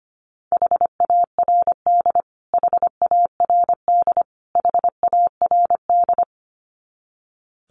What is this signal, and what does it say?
Morse code, heard as a keyed tone, about 25 wpm, 700 Hz. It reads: HARD HARD HARD
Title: Top 300 Words Repeated 25wpm